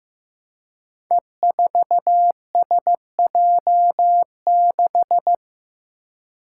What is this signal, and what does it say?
Morse code, heard as a keyed tone, about 15 wpm, 700 Hz. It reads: E4SJ6
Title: Five Challenging Letter-Number Combinations 15wpm